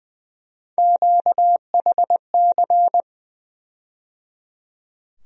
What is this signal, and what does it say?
Morse code, heard as a keyed tone, about 21 wpm, 700 Hz. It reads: QHC
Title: USA Call Sign Postfixes - 3 Letters 20wpm